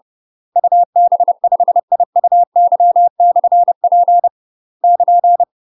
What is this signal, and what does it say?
Morse code, heard as a keyed tone, about 30 wpm, 700 Hz. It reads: UB5IUY/P <KN>
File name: UK MASTER QSOs TRAINING / 30WPM / UK MASTER QSOs 0049